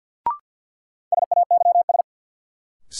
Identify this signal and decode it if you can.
Morse code, heard as a keyed tone, about 53 wpm, 700 Hz. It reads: SAYS
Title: Sets of 2 Words - New Word Drill - 50wpm